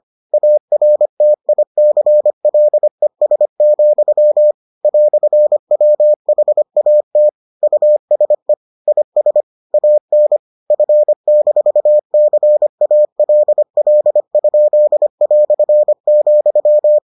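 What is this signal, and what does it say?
Morse code, heard as a keyed tone, about 25 wpm, 600 Hz. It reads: ARTICLES, "WHAT USE IS AN F-CALL?",
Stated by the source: Morse edition of the Foundations of Amateur Radio episodes.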